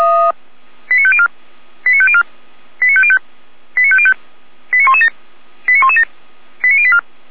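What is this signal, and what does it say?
Five-tone selective call (ZVEI), heard as quick runs of five tones, tones of about 70 ms. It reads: 89504 89504 89505 89505 89117 89117 89084
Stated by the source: ZVEI five-tone call received by SDR.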